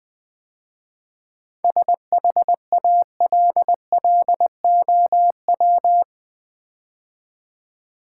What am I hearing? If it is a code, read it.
Morse code, heard as a keyed tone, about 20 wpm, 700 Hz. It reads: SHALLOW
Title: 7 Letter Words - First 3 Letters Given - 20wpm